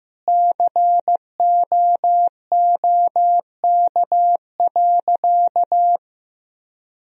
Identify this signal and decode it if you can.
Morse code, heard as a keyed tone, about 15 wpm, 700 Hz. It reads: COOK.
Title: Sentences from Top 1000 Words - Review Part 4 - 15wpm